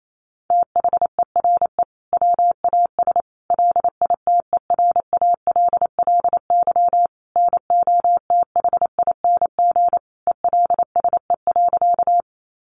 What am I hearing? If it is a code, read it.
Morse code, heard as a keyed tone, about 28 wpm, 700 Hz. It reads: THERE WAS LITERALLY NOTHING ELSE.
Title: Sentences from Top 1400 Words 28wpm